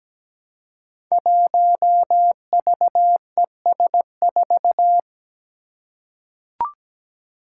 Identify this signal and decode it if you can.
Morse code, heard as a keyed tone, about 17 wpm, 700 Hz. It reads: 1VES4
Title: Five Challenging Letter-Number Combinations 17wpm